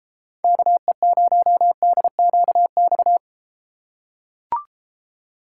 Morse code, heard as a keyed tone, about 31 wpm, 700 Hz. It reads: KE0DQX